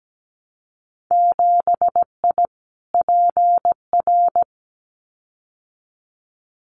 Morse code, heard as a keyed tone, about 17 wpm, 700 Hz. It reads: 7I PR
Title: Winter Field Day Exchanges Encore 17wpm